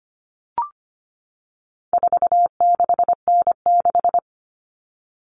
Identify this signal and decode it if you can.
Morse code, heard as a keyed tone, about 25 wpm, 700 Hz. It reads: 46N6
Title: Common Cut Numbers - 4-Digit 25wpm